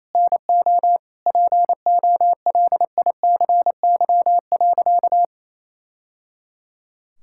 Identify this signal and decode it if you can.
Morse code, heard as a keyed tone, about 28 wpm, 700 Hz. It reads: NO POLICY.
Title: Sentences from Top 1000 Words - Review Part 5 - 28wpm